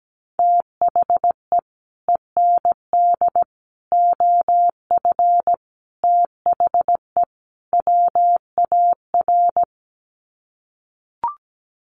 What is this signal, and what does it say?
Morse code, heard as a keyed tone, about 17 wpm, 700 Hz. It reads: THE END OF THE WAR
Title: Sets of 5 Words 17wpm